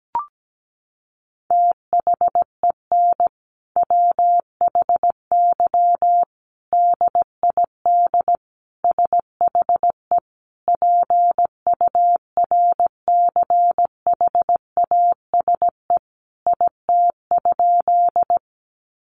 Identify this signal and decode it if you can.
Morse code, heard as a keyed tone, about 17 wpm, 700 Hz. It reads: THEN WHY DID SHE PURCHASE IT?
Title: Sentences from Top 800 Words 17wpm